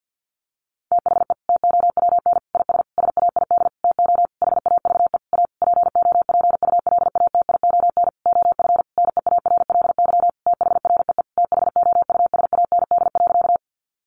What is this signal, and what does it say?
Morse code, heard as a keyed tone, about 50 wpm, 700 Hz. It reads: THE TOWN IS SAID TO HAVE A POPULATION OF NEARLY THREE THOUSAND.
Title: Sentences from Top 1300 Words 50wpm